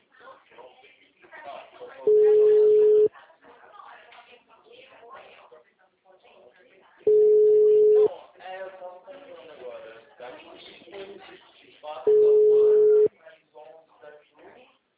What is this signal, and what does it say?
Ringback tone (Europe, 425 Hz), with speech.